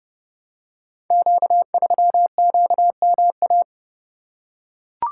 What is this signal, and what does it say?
Morse code, heard as a keyed tone, about 30 wpm, 700 Hz. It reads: Q3QMA
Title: Code Groups - 5 Letter:Numbers - 30wpm